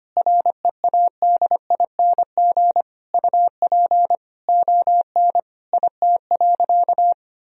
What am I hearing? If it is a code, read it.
Morse code, heard as a keyed tone, about 25 wpm, 700 Hz. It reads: READING UP ON IT.
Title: Sentences from Top 1000 Words - Review Part 3 - 25wpm